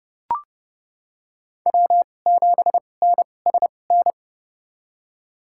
Morse code, heard as a keyed tone, about 29 wpm, spaced as for 21 wpm, 700 Hz. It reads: W7NSN
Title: USA Call Signs - 5 Characters - 2x Character Spacing - 30wpm